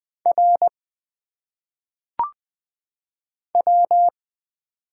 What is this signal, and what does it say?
Morse code, heard as a keyed tone, about 20 wpm, 700 Hz. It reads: R W
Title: Single Letters - Phonetic Pronunciation 20wpm